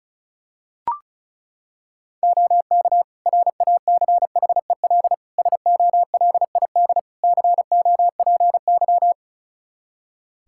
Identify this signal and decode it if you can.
Morse code, heard as a keyed tone, about 36 wpm, 700 Hz. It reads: OK RACHEL SOLID COPY
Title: K1USN SST QSO Practice 35wpm